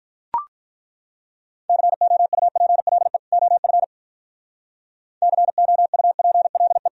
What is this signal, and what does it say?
Morse code, heard as a keyed tone, about 42 wpm, 700 Hz. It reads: COUPLE OF COUPLE
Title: Speed-Racing - Sets of 2 Words 35wpm